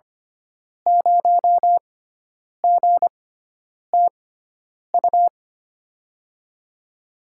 Morse code, heard as a keyed tone, about 25 wpm, spaced as for 9 wpm, 700 Hz. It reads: 0GTU
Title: USA Call Signs - 6 Characters - 6x Character Spacing - 25wpm